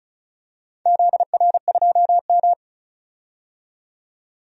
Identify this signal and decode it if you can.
Morse code, heard as a keyed tone, about 36 wpm, 700 Hz. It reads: ZR2M
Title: DXCC-100 Call Signs - 4 Characters - 35wpm